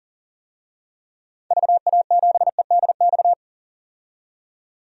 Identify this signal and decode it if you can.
Morse code, heard as a keyed tone, about 40 wpm, 700 Hz. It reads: VA7EDX
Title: DXCC-100 Call Signs - 6 Characters - 40wpm